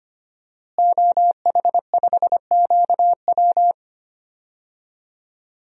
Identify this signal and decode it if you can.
Morse code, heard as a keyed tone, about 25 wpm, 700 Hz. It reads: OH5QW
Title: DXCC-100 Call Signs - 5 Characters - Repeated 2x - 25wpm